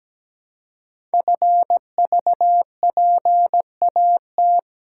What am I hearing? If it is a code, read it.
Morse code, heard as a keyed tone, about 17 wpm, 700 Hz. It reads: FVPAT
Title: Code Groups - 5 Letters - 17wpm